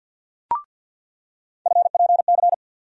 Morse code, heard as a keyed tone, about 47 wpm, 700 Hz. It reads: FPC
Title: USA Call Sign Postfixes - 3 Letters 50wpm